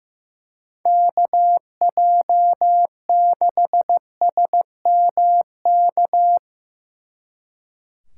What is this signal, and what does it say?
Morse code, heard as a keyed tone, about 17 wpm, 700 Hz. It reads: KJ6SMK
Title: Speed Racing - USA 2x3 Call Signs 15wpm